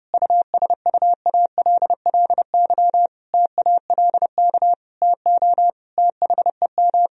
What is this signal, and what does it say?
Morse code, heard as a keyed tone, about 30 wpm, 700 Hz. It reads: USUALLY TALK TO THEM
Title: Sentences from Top 1000 Words - Review Part 3 - 30wpm